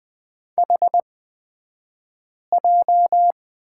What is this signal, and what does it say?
Morse code, heard as a keyed tone, about 20 wpm, spaced as for 5 wpm, 700 Hz. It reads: HJ
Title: 223 Lesson 36d - BT - Two Characters 20wpm